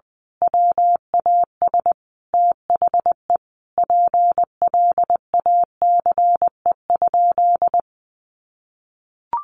Morse code, heard as a keyed tone, about 20 wpm, 700 Hz. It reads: WAS THE PLACE?
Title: Sentences from Top 200 Words 20wpm